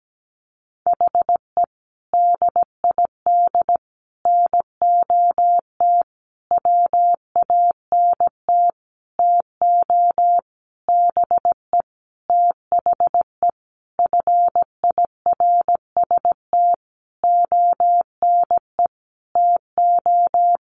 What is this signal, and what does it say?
Morse code, heard as a keyed tone, about 17 wpm, 700 Hz. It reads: HE DID NOT WANT TO BE THE FIRST ONE TO
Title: Sentences from Top 1300 Words 17wpm